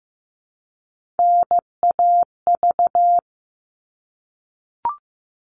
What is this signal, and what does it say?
Morse code, heard as a keyed tone, about 15 wpm, 700 Hz. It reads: NAV
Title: All Cut Numbers - 3-Digit 15wpm